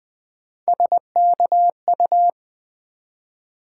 Morse code, heard as a keyed tone, about 20 wpm, 700 Hz. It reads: SKU
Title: Build A Word - 1 Letter At A Time - 5-Letter Words - 20wpm